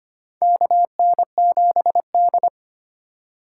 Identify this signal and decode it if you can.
Morse code, heard as a keyed tone, about 25 wpm, 700 Hz. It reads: KN7D
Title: K1USN SST QSO Practice - 25wpm